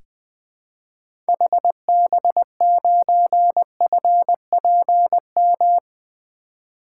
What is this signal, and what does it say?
Morse code, heard as a keyed tone, about 20 wpm, 700 Hz. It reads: HB9FPM
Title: 2023 SOTA Activator Call Signs - 20wpm